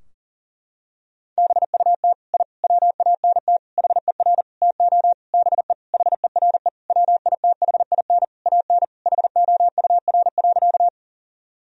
Morse code, heard as a keyed tone, about 40 wpm, 700 Hz. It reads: BUT I WANT HER TO BE HERE WITHIN AN HOUR.
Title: Sentences from Top 1000 Words - Review Part 4 - 40wpm